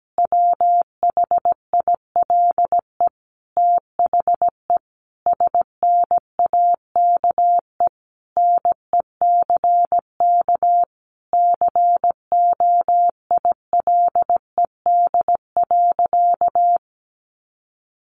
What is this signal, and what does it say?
Morse code, heard as a keyed tone, about 17 wpm, 700 Hz. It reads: WHILE THE SNAKE NECK COILED.
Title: A Sound Of Thunder 17wpm